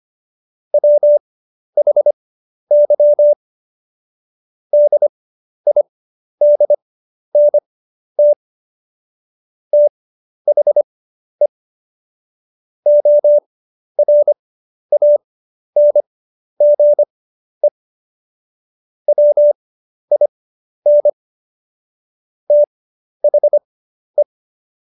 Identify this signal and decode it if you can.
Morse code, heard as a keyed tone, about 25 wpm, spaced as for 11 wpm, 590 Hz. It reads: WHY DIDNT THE ORANGE WIN THE
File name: Jokes-25-char-10-eff